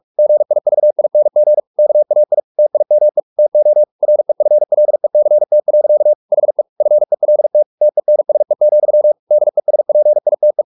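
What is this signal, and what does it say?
Morse code, heard as a keyed tone, about 44 wpm, 600 Hz. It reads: GIVING KAI TIME TO REFLECT. HE FELT TENSE, DESPITE
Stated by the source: Morse practice at 45 wpm, the novel Never by Ken Follett in English, from CW med Gnister.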